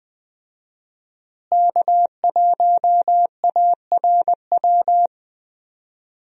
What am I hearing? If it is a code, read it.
Morse code, heard as a keyed tone, about 20 wpm, 700 Hz. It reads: K1ARW
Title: USA Call Signs - 5 Characters - Repeated 3x - 20wpm